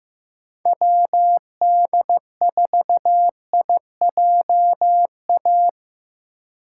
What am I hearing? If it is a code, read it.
Morse code, heard as a keyed tone, about 15 wpm, 700 Hz. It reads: WD4IJA
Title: USA Call Signs - 6 Characters - 4x Character Spacing - 15wpm